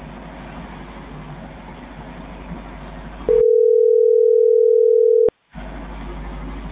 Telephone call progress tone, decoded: ringback tone

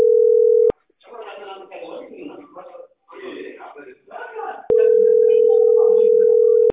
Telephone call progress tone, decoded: ringback tone

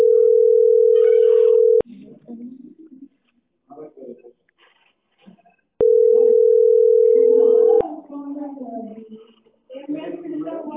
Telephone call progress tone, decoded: ringback tone

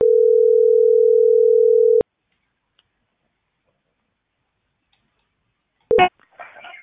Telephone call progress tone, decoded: ringback tone

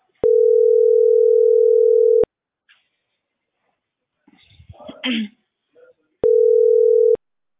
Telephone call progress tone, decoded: ringback tone

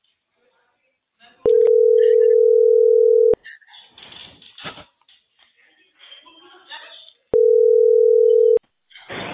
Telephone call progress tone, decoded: ringback tone

